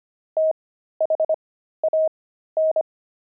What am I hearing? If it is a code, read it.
Morse code, heard as a keyed tone, about 25 wpm, spaced as for 13 wpm, 630 Hz. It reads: THAN
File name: pangram-fs-13-cs-25-sissy